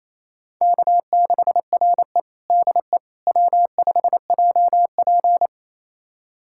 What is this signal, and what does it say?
Morse code, heard as a keyed tone, about 28 wpm, 700 Hz. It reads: K6RE DE W5JP
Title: USA Callsign DE Callsign - 4 Characters - 28wpm